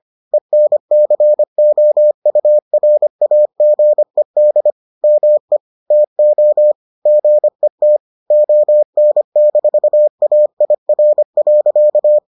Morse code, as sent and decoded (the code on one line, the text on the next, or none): . -. -.-. --- ..- .-. .- --. -.. / -- . / - --- / --. - / --- -. -....- .- .. .-. .-.-.-
ENCOURAGED ME TO GET ON-AIR.